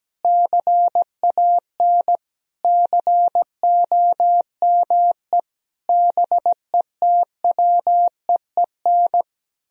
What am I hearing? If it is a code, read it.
Morse code, heard as a keyed tone, about 17 wpm, 700 Hz. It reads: CAN COME BETWEEN
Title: Sentences from Top 1000 Words - Review Part 1 - 17wpm